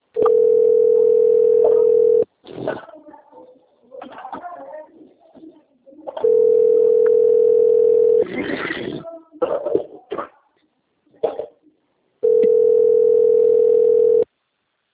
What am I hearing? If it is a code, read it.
Ringback tone (North America), 440 + 480 Hz.
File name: rg-1600-unknown- - - .446